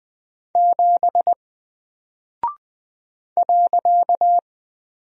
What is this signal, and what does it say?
Morse code, heard as a keyed tone, about 20 wpm, 700 Hz. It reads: 7 .
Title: Single Character (Letter:Number:Punctuation:Prosign) Alt 20wpm